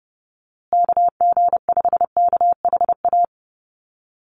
Morse code, heard as a keyed tone, about 30 wpm, 700 Hz. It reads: KG5KHA
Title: USA Call Signs Repeated 3x 30wpm